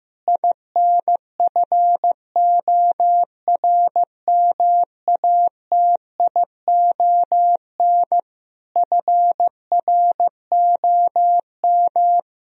Morse code, heard as a keyed tone, about 15 wpm, 700 Hz. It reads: INFORMATION FROM
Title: Sentences from Top 1000 Words - Review Part 3 - 15wpm